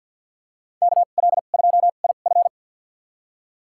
Morse code, heard as a keyed tone, about 52 wpm, spaced as for 36 wpm, 700 Hz. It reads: KL2IF
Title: USA Call Signs - 5 Characters - 2x Character Spacing - 50wpm